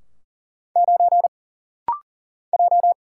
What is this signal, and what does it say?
Morse code, heard as a keyed tone, about 38 wpm, 700 Hz. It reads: 9 J
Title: Single Letter-Number - Rapid-Fire 40wpm